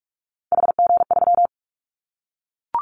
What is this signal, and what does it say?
Morse code, heard as a keyed tone, about 44 wpm, 700 Hz. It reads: HG3
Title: Callsign Prefixes for Top 25 DXCC - 45wpm